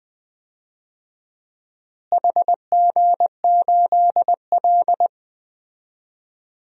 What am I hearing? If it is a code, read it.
Morse code, heard as a keyed tone, about 20 wpm, 690 Hz. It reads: HG8L
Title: DXCC-100 Call Signs - 4 Characters - Repeated 3x - 20wpm